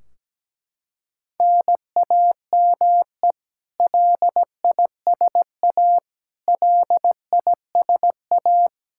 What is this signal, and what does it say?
Morse code, heard as a keyed tone, about 17 wpm, 700 Hz. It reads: NAME LISA LISA
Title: Practice QSO - Exchange 1 - Beginning - 17wpm